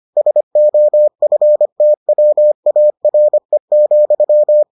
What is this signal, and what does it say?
Morse code, heard as a keyed tone, about 25 wpm, 600 Hz. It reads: SOFTWARE,